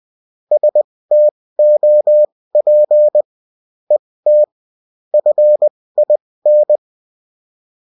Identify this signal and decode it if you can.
Morse code, heard as a keyed tone, about 20 wpm, spaced as for 16 wpm, 600 Hz. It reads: STOP ET FIN